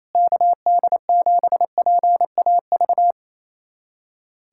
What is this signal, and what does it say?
Morse code, heard as a keyed tone, about 28 wpm, 700 Hz. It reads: KD7PAV
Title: USA Callsign DE Callsign 28wpm